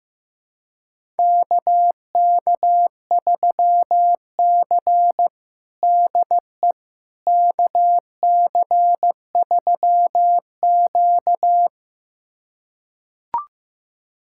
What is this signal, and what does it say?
Morse code, heard as a keyed tone, about 15 wpm, 700 Hz. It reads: KK3C DE KC3Q
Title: USA Callsign DE Callsign - 4 Characters - 15wpm